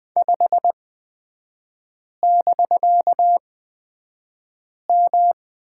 Morse code, heard as a keyed tone, about 20 wpm, spaced as for 5 wpm, 700 Hz. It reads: 5<BK>M
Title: 250 Lesson 40e - BK - Three Characters 20wpm